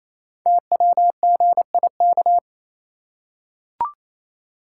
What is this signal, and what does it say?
Morse code, heard as a keyed tone, about 29 wpm, 700 Hz. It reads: TWGIK
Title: Code Groups - 5 Letters - 28wpm